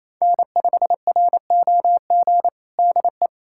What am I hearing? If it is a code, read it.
Morse code, heard as a keyed tone, about 28 wpm, 700 Hz. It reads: N5ROG DE
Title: USA Callsign DE Callsign - 5 Characters - 28wpm